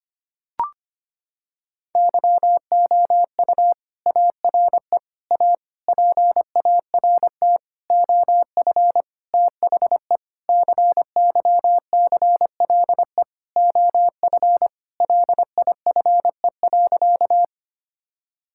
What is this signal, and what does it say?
Morse code, heard as a keyed tone, about 25 wpm, 700 Hz. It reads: YOU ARE A PART OF THE CYCLE OF LIFE.
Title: Sentences from Top 1000 Words - Review Part 6 - 25wpm